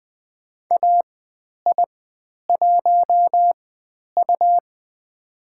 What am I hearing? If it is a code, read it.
Morse code, heard as a keyed tone, about 20 wpm, spaced as for 10 wpm, 700 Hz. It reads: AI1U
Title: 102 Lesson 17g - Letter u - Call Signs 20wpm